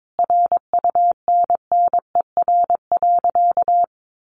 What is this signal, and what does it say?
Morse code, heard as a keyed tone, about 22 wpm, 700 Hz. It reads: RUNNER.
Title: Additional Easy Phrases 22wpm